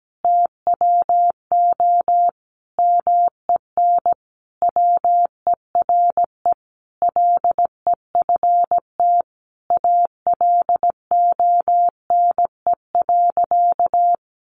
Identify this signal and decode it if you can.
Morse code, heard as a keyed tone, about 17 wpm, 700 Hz. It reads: TWO MEN WERE LEFT ALONE.
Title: Sentences from Top 1000 Words 17wpm